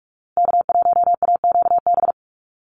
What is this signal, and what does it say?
Morse code, heard as a keyed tone, about 44 wpm, 700 Hz. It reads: K1AQB
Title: Call Signs Repeated 2x - 45wpm